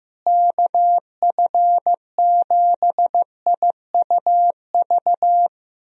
Morse code, heard as a keyed tone, about 15 wpm, 700 Hz. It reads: KF7IUV
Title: USA Call Signs - 6 Characters - Repeated 2x - 15wpm